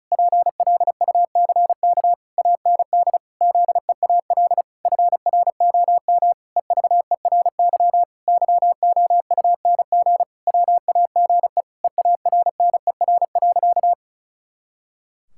Morse code, heard as a keyed tone, about 36 wpm, 700 Hz. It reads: PLUCK AND ZEAL FROM EVERY YOUNG WAGE EARNER.
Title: Pangram Challenge v2 - 35wpm